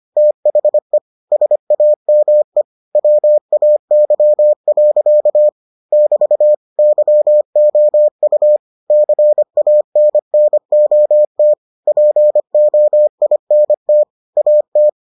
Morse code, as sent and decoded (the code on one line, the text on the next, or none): - .... / ... .- -- . / .-- .- -.-- .-.-.- / -...- / -.-- --- ..- / -.-. .- -. -. --- - / .--. --- .. -. - / .- -
THE SAME WAY. <BT> YOU CANNOT POINT AT